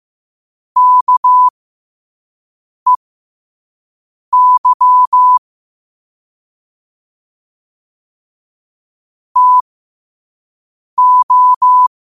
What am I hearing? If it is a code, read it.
Morse code, heard as a keyed tone, about 15 wpm, spaced as for 5 wpm, 1000 Hz. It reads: KEY TO
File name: qotdfiles / 5wpm